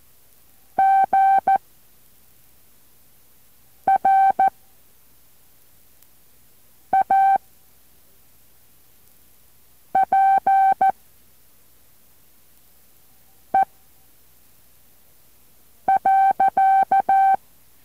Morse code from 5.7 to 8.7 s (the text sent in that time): A